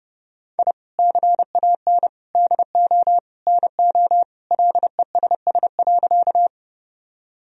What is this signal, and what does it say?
Morse code, heard as a keyed tone, about 32 wpm, 700 Hz. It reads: I CAN DO NO LESS.
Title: Sentences from Top 1000 Words - Review Part 2 - 30wpm